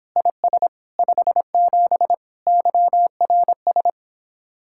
Morse code, heard as a keyed tone, about 31 wpm, 700 Hz. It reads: IS 57 YRS